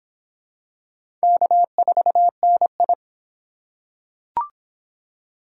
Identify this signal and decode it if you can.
Morse code, heard as a keyed tone, about 22 wpm, 700 Hz. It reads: K4NI